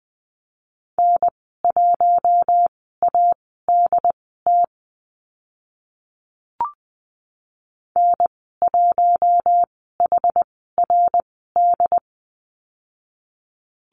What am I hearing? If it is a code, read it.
Morse code, heard as a keyed tone, about 20 wpm, spaced as for 14 wpm, 700 Hz. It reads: N1ADT N1HRD
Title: 069 Lesson 12g - Letter d - Call Signs 20wpm